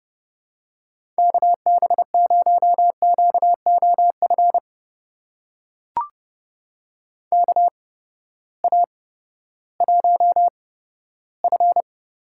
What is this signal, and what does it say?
Morse code, heard as a keyed tone, about 30 wpm, spaced as for 15 wpm, 700 Hz. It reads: KB0QOF KA1F
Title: USA Call Signs - 6 Characters - 8x Character Spacing - 30wpm